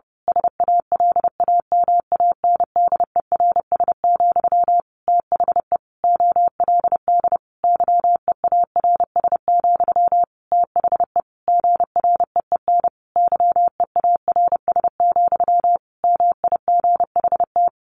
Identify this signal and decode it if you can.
Morse code, heard as a keyed tone, about 30 wpm, 700 Hz. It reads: SALAMANDERS, THE OLD YEARS, THE GREEN YEARS, MIGHT
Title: A Sound Of Thunder 30wpm